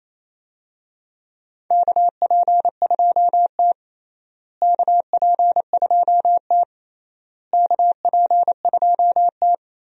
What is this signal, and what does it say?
Morse code, heard as a keyed tone, about 28 wpm, 700 Hz. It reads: KP2T KP2T KP2T
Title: DXCC-100 Call Signs - 4 Characters - Repeated 3x - 28wpm